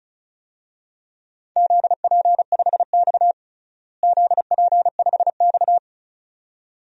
Morse code, heard as a keyed tone, about 36 wpm, 700 Hz. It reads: ZP5X ZP5X
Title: DXCC-100 Call Signs - 4 Characters - Repeated 2x - 35wpm